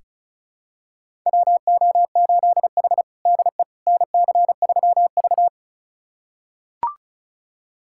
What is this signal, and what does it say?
Morse code, heard as a keyed tone, about 36 wpm, 700 Hz. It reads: WO8H DE NC3V
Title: USA Callsign DE Callsign - 4 Characters - 35wpm